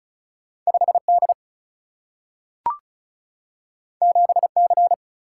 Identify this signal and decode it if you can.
Morse code, heard as a keyed tone, about 36 wpm, 700 Hz. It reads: 5 D 7 C
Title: Single-Letter Single-Number 35wpm